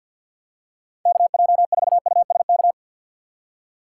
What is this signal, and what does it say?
Morse code, heard as a keyed tone, about 49 wpm, 700 Hz. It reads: KJ4USK
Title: USA Call Signs - 6 Characters - 4x Character Spacing - 50wpm